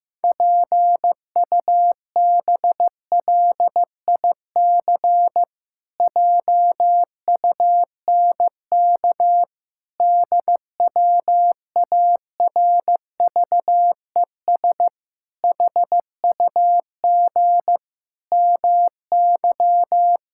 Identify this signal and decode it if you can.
Morse code, heard as a keyed tone, about 15 wpm, 700 Hz. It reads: PUBLIC JUNK DWARVES HUG MY
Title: Pangram Challenge v2 - 15wpm